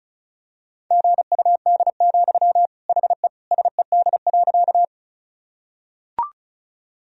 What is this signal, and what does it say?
Morse code, heard as a keyed tone, about 36 wpm, 700 Hz. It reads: GUD, HE SED.
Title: Sentences from Top 200 Words and Abbreviations 35wpm